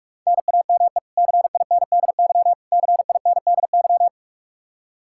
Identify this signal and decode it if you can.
Morse code, heard as a keyed tone, about 45 wpm, 700 Hz. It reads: NAME CINDY CINDY
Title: K1USN SST QSO Practice 45wpm